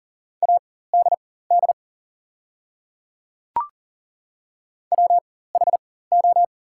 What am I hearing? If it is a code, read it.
Morse code, heard as a keyed tone, about 40 wpm, spaced as for 19 wpm, 700 Hz. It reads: ADD WHO
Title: 4x Character Spacing - Top 300 Words - 40wpm